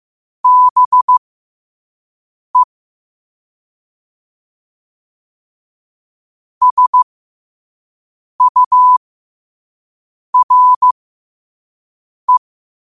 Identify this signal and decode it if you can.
Morse code, heard as a keyed tone, about 15 wpm, spaced as for 5 wpm, 1000 Hz. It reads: BE SURE
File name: qotdfiles / 5wpm